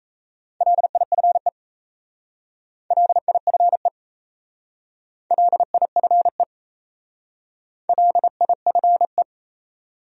Morse code, heard as a keyed tone, about 35 wpm, 700 Hz. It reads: LIFE LIFE LIFE LIFE